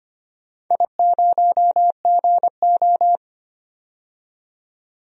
Morse code, heard as a keyed tone, about 25 wpm, 700 Hz. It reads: I0GO
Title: DXCC-100 Call Signs - 4 Characters - 25wpm